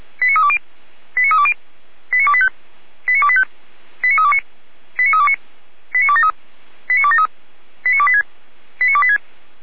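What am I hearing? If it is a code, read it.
Five-tone selective call (ZVEI), heard as quick runs of five tones, tones of about 70 ms. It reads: 89320 89320 89285 89285 89329 89329 89273 89273 89286 89286
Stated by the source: ZVEI five-tone selective call received by SDR, from the fire brigade Feuerwehr Orlamünde.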